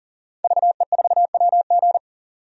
Morse code, heard as a keyed tone, about 40 wpm, 700 Hz. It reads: VE4WG
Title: DXCC-100 Call Signs - 5 Characters - 40wpm